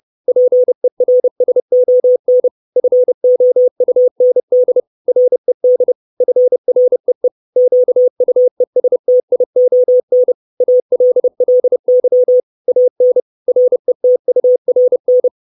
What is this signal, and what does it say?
Morse code, heard as a keyed tone, about 30 wpm, 500 Hz. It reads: PERSON FOUND RED FREE QUESTION ALLY AN RETURN
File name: 632 Words 30WPM